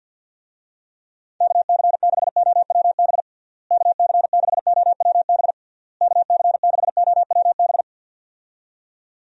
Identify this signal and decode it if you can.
Morse code, heard as a keyed tone, about 51 wpm, 700 Hz. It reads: KC6OWB KC6OWB KC6OWB
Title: USA Call Signs - 6 Characters - Repeated 3x - 50wpm